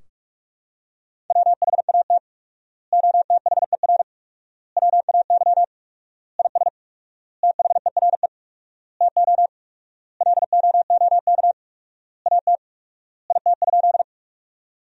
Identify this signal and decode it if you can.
Morse code, heard as a keyed tone, about 46 wpm, 700 Hz. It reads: WHAT OTHER WAY IS THERE TO LOOK AT IT?
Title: 4x Word-Spacing - Sentences from Top 100 Words - 45wpm